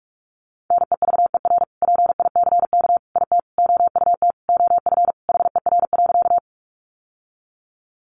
Morse code, heard as a keyed tone, about 44 wpm, 700 Hz. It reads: NEVER PICK IT OUT OF HER.
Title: Sentences from Top 1000 Words 45wpm